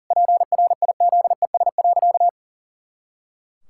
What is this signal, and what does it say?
Morse code, heard as a keyed tone, about 37 wpm, 700 Hz. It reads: PRIZES.
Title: Pangram Challenge v2 - 40wpm